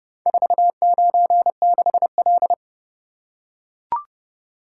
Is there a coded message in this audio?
Yes: Morse code from 0.3 to 2.5 s, heard as a keyed tone, about 30 wpm, 700 Hz. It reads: 496L